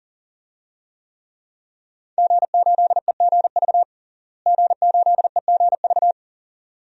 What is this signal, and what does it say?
Morse code, heard as a keyed tone, about 40 wpm, 690 Hz. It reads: G8EGV G8EGV
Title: DXCC-100 Call Signs - 5 Characters - Repeated 2x - 40wpm